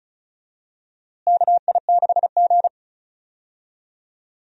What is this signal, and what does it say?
Morse code, heard as a keyed tone, about 36 wpm, 700 Hz. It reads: KI6G
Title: USA Call Signs - 4 Characters - Repeated 3x - 35wpm